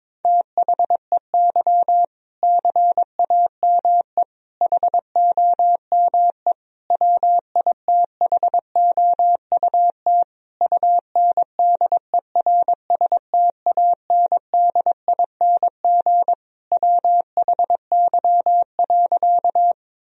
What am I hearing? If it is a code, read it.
Morse code, heard as a keyed tone, about 22 wpm, 700 Hz. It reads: THEY CAME HOME WITHOUT UNDERSTANDING WHY.
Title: Sentences from Top 1000 Words - Review Part 3 - 22wpm